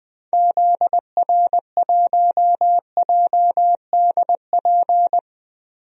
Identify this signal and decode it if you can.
Morse code, heard as a keyed tone, about 21 wpm, 700 Hz. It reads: ZR1JDP
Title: DXCC-100 Call Signs - 6 Characters - Repeated 2x - 20wpm